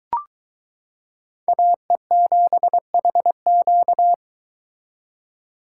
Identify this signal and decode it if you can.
Morse code, heard as a keyed tone, about 22 wpm, 700 Hz. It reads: AE7HQ